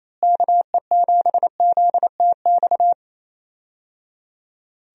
Morse code, heard as a keyed tone, about 28 wpm, 700 Hz. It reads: KE7ZTX
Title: USA Call Signs - 6 Characters - Repeated 3x - 28wpm